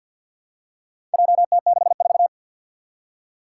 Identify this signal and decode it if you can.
Morse code, heard as a keyed tone, about 47 wpm, 700 Hz. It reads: JT64
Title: Four Challenging Letter-Number Combinations 50wpm